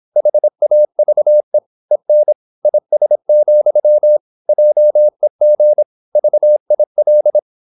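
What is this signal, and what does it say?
Morse code, heard as a keyed tone, about 26 wpm, 600 Hz. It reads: HAVE EN IS, JEG VIL
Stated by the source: High-speed (QRQ) Morse at 26 wpm.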